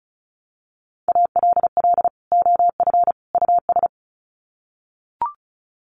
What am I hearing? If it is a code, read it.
Morse code, heard as a keyed tone, about 34 wpm, 700 Hz. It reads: ALL OF US
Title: Sets of 3 Words 35wpm